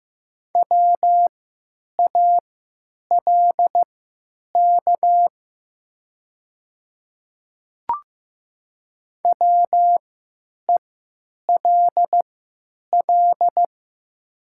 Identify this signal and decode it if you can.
Morse code, heard as a keyed tone, about 15 wpm, spaced as for 9 wpm, 700 Hz. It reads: WALK WELL
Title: 3x Character Spacing - Top 300 Words - 15wpm